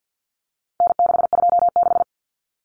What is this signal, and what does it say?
Morse code, heard as a keyed tone, about 50 wpm, 700 Hz. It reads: N626
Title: Common Cut Numbers - 4-Digit 50wpm